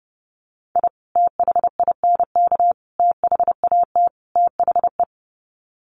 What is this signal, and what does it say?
Morse code, heard as a keyed tone, about 30 wpm, 700 Hz. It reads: I THINK THAT THE
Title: Sets of 4 Words 30wpm